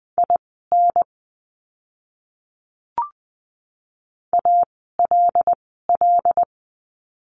Morse code, heard as a keyed tone, about 20 wpm, spaced as for 15 wpm, 700 Hz. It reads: IN ALL
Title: 101 Lesson 17f - Letter u - Words 20wpm